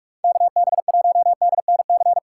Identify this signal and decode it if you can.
Morse code, heard as a keyed tone, about 43 wpm, 700 Hz. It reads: KB1DNC
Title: USA Call Signs - 6 Characters - 8x Character Spacing - 45wpm